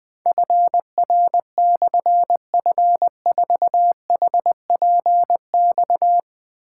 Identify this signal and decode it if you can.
Morse code, heard as a keyed tone, about 20 wpm, 700 Hz. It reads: FR/F4HPX
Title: Difficult Call Signs 20wpm